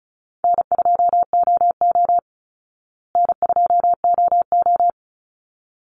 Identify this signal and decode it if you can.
Morse code, heard as a keyed tone, about 34 wpm, 700 Hz. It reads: N2OO N2OO
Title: Call Signs Repeated 3x - 35wpm